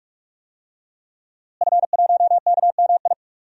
Morse code, heard as a keyed tone, about 48 wpm, 700 Hz. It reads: F1KMI
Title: DXCC-100 Call Signs - 5 Characters - Repeated 2x - 45wpm